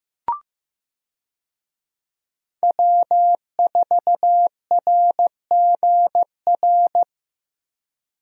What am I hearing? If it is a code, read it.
Morse code, heard as a keyed tone, about 16 wpm, 700 Hz. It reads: W4RGR
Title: USA Call Signs - 5 Characters - Repeated 3x - 15wpm